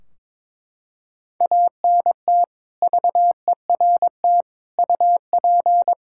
Morse code, heard as a keyed tone, about 22 wpm, 700 Hz. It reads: ANT VERT UP
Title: Practice QSO - Exchange 2 - Info - 22wpm